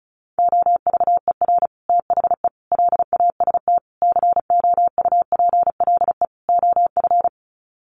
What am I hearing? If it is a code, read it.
Morse code, heard as a keyed tone, about 34 wpm, 700 Hz. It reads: OVER THE LAST COUPLE OF
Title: Sets of 5 Words Encore 35wpm